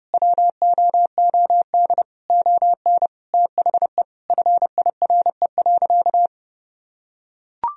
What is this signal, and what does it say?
Morse code, heard as a keyed tone, about 32 wpm, 700 Hz. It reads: WOOD ON THE FIRE.
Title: Sentences from Top 1000 Words - Review Part 2 - 30wpm